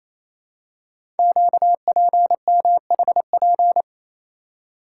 Morse code, heard as a keyed tone, about 27 wpm, 700 Hz. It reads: QPMHP
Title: Code Groups - 5 Letters - 28wpm